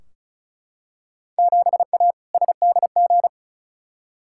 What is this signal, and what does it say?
Morse code, heard as a keyed tone, about 36 wpm, 690 Hz. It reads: 7A SDG
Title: Field Day Exchanges Encore - 35wpm